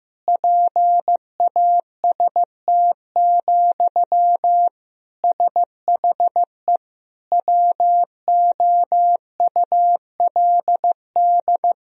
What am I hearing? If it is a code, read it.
Morse code, heard as a keyed tone, about 16 wpm, 700 Hz. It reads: PAST, SHE WOULD
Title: Sentences from Top 1000 Words - Review Part 3 - 15wpm